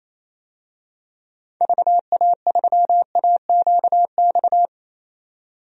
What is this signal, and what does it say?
Morse code, heard as a keyed tone, about 28 wpm, 700 Hz. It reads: VA3AQX
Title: DXCC-100 Call Signs - 6 Characters - 28wpm